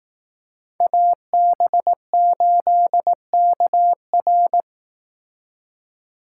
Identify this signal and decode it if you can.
Morse code, heard as a keyed tone, about 18 wpm, 700 Hz. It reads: AB8KR
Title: Speed Racing - USA 2x2 Call Signs 15wpm